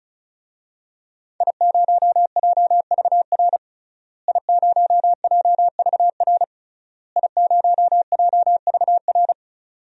Morse code, heard as a keyed tone, about 36 wpm, 700 Hz. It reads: I0JVR I0JVR I0JVR
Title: DXCC-100 Call Signs - 5 Characters - Repeated 3x - 35wpm